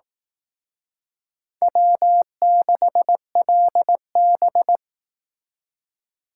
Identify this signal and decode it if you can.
Morse code, heard as a keyed tone, about 18 wpm, 700 Hz. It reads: W6LB
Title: Speed Racing - USA 1x2 Call Signs 15wpm